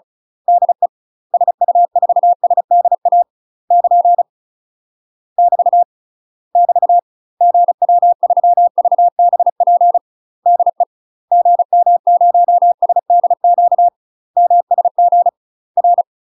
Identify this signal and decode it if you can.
Morse code, heard as a keyed tone, about 35 wpm, 700 Hz. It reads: DE SU4SDA <KN> <BT> <BT> GW3VBP DE GM0SDQ MSG R
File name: UK MASTER QSOs TRAINING / 35WPM / UK MASTER QSOs 0036